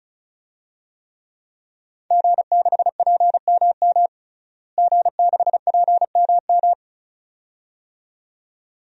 Morse code, heard as a keyed tone, about 34 wpm, 700 Hz. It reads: G6PMM G6PMM
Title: DXCC-100 Call Signs - 5 Characters - Repeated 2x - 35wpm